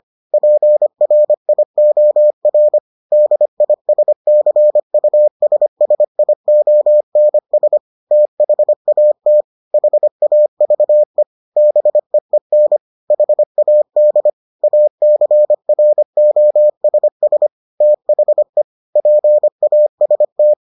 Morse code, heard as a keyed tone, about 25 wpm, 600 Hz. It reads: PRIOR DISCUSSIONS THAT HAVE BEEN HAD ACROSS THE PAST